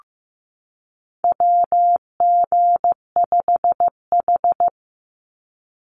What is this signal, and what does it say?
Morse code, heard as a keyed tone, about 15 wpm, 700 Hz. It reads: WG5H
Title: Call Signs Repeated 3x - 15wpm